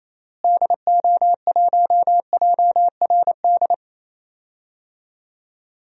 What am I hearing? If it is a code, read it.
Morse code, heard as a keyed tone, about 28 wpm, 700 Hz. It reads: DO1JRD
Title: DXCC-100 Call Signs - 6 Characters - Repeated 3x - 28wpm